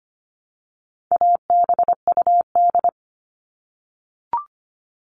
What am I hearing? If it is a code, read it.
Morse code, heard as a keyed tone, about 25 wpm, 700 Hz. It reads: ABUD
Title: All Cut Numbers - 4-Digit 25wpm